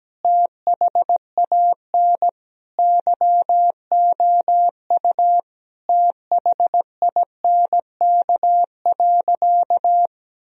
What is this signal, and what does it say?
Morse code, heard as a keyed tone, about 17 wpm, 700 Hz. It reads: THAN YOU THINK.
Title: Sentences from Top 1000 Words - Review Part 3 - 17wpm